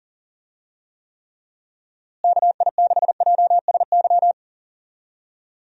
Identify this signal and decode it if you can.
Morse code, heard as a keyed tone, about 37 wpm, 700 Hz. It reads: KI6JSY
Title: USA Call Signs - 6 Characters - Repeated 2x - 40wpm